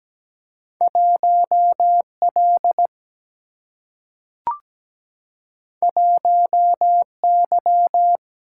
Morse code, heard as a keyed tone, about 17 wpm, 700 Hz. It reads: 1L 1Y
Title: Single-Letter Single-Number 17wpm